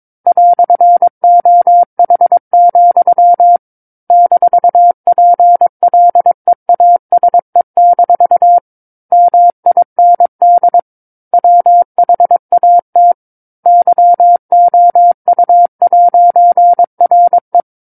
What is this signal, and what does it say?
Morse code, heard as a keyed tone, about 22 wpm, 700 Hz. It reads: "OH, -PLEASE- MIND WHAT YOU'RE
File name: 06-Chapter-06 22 WPM